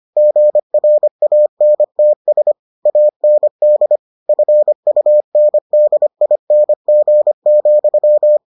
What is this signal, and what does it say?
Morse code, heard as a keyed tone, about 25 wpm, 600 Hz. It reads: GRANTS AND FUNDING,